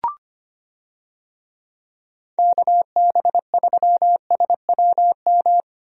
Morse code, heard as a keyed tone, about 25 wpm, 700 Hz. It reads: KB3SWM
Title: USA Call Signs - 6 Characters - Repeated 3x - 25wpm